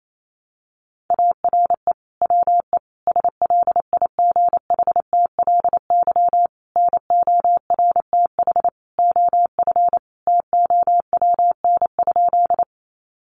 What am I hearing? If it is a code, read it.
Morse code, heard as a keyed tone, about 28 wpm, 700 Hz. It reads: ARE WE SLIGHTLY NORTH OF TOWN?
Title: Sentences from Top 1300 Words 28wpm